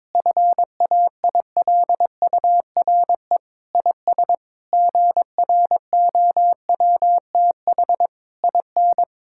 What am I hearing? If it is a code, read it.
Morse code, heard as a keyed tone, about 22 wpm, 700 Hz. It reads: FAILURE IS GROWTH IN
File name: Difficult Sentences with Hints - 22wpm